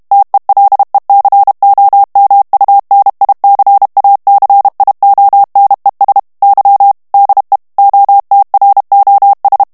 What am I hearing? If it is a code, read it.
Morse code, heard as a keyed tone, about 32 wpm, 800 Hz. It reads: TELECOMUNICACIONES Y DE OTROS
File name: morse mp3 / ESPAGNOL TEXTE EN CLAIR 2 VITESSE 30 MOTS MINUTES